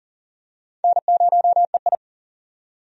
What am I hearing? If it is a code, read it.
Morse code, heard as a keyed tone, about 40 wpm, 700 Hz. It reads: N0EI
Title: USA Call Signs - 4 Characters - 40wpm